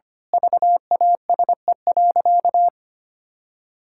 Morse code, heard as a keyed tone, about 25 wpm, 700 Hz. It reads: VASE.
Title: Pangram Challenge v2 - 25wpm